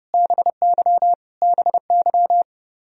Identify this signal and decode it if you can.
Morse code, heard as a keyed tone, about 29 wpm, 700 Hz. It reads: BY BY
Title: Top 100 Words Repeated 30wpm